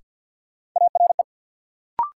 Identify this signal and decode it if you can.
Morse code, heard as a keyed tone, about 53 wpm, 700 Hz. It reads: ARE
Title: Sentences from Top 100 Words - New Word Drill - 50wpm